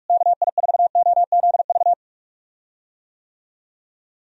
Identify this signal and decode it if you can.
Morse code, heard as a keyed tone, about 41 wpm, 700 Hz. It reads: KI4OZV
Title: USA Call Signs - 6 Characters - Repeated 3x - 45wpm